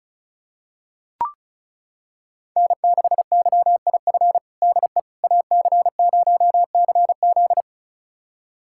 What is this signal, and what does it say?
Morse code, heard as a keyed tone, about 34 wpm, 700 Hz. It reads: N6YIF DE AC0CZ
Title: USA Callsign DE Callsign - 5 Characters - 35wpm